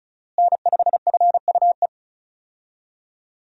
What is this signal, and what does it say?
Morse code, heard as a keyed tone, about 36 wpm, 700 Hz. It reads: N5FUE
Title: USA Call Signs - 5 Characters - Repeated 2x - 35wpm